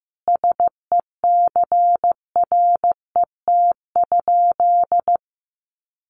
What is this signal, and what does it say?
Morse code, heard as a keyed tone, about 15 wpm, 700 Hz. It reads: SECRET?
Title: Sentences from Top 1200 Words 15wpm